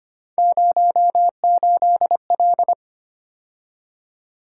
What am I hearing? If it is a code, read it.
Morse code, heard as a keyed tone, about 25 wpm, 700 Hz. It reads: 08L
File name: Three Letter-Number Combinations - 25wpm